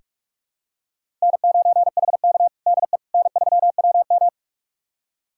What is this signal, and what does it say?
Morse code, heard as a keyed tone, about 46 wpm, 700 Hz. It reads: N9HK DE N3WM
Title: USA Callsign DE Callsign - 4 Characters - 45wpm